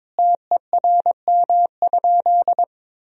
Morse code, heard as a keyed tone, about 22 wpm, 700 Hz. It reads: TERM?
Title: Sentences from Top 1000 Words - Review Part 4 - 22wpm